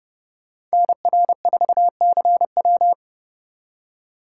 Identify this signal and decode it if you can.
Morse code, heard as a keyed tone, about 38 wpm, 700 Hz. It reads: NR4CW